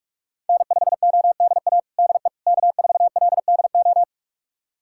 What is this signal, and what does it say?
Morse code, heard as a keyed tone, about 46 wpm, 700 Hz. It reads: N5ODA DE K4LDO
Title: USA Callsign DE Callsign - 5 Characters - 45wpm